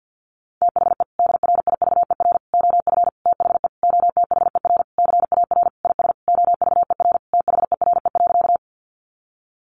Morse code, heard as a keyed tone, about 50 wpm, 700 Hz. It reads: THE DRIVER OF THE OTHER CAR IS OVER THERE.
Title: Sentences from Top 1000 Words 50wpm